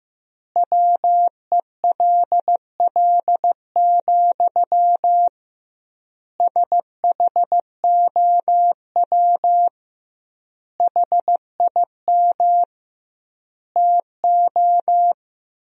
Morse code, heard as a keyed tone, about 15 wpm, 700 Hz. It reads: WELL, SHOW HIM TO
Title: 2x Word-Spacing - Sentences from Top 200 Words - 15wpm